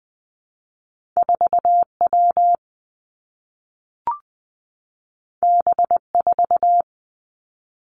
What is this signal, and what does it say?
Morse code, heard as a keyed tone, about 20 wpm, 700 Hz. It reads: 4 W B 4
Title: Two Challenging Letter-Number Combinations 20wpm